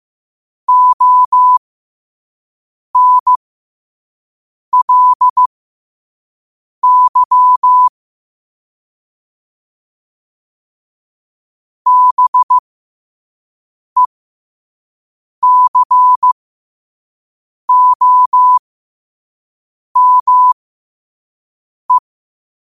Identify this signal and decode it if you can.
Morse code, heard as a keyed tone, about 15 wpm, spaced as for 5 wpm, 1000 Hz. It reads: ONLY BECOME
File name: qotdfiles / 5wpm